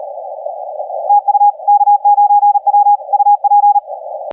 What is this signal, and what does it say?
Morse code, heard as a keyed tone, about 34 wpm, 820 Hz, with radio noise. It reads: TU K9W UP
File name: K9W 0917UTC 17m